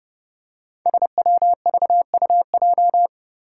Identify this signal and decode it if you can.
Morse code, heard as a keyed tone, about 28 wpm, 700 Hz. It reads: SWVUJ
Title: Five Challenging Letter-Number Combinations 30wpm